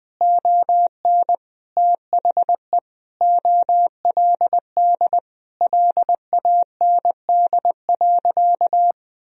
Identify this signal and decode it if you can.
Morse code, heard as a keyed tone, about 20 wpm, 700 Hz. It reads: ON THE OLD LAND.
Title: Sentences from Top 1000 Words - Review Part 1 - 20wpm